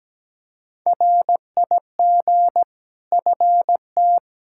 Morse code, heard as a keyed tone, about 17 wpm, 700 Hz. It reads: RIG FT
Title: K1USN SST QSO Practice 17wpm